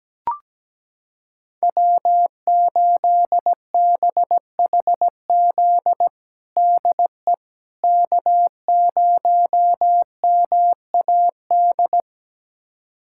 Morse code, heard as a keyed tone, about 17 wpm, 700 Hz. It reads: W8BHZ DE K0MAD
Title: USA Callsign DE Callsign - 5 Characters - 17wpm